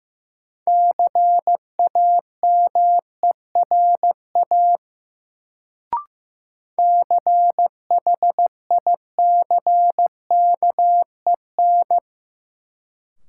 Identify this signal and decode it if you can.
Morse code, heard as a keyed tone, about 15 wpm, 700 Hz. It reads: CAMERA CHICKEN
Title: Sentences from Top 1000 Words - Review Part 5 - New Word Drill - 15wpm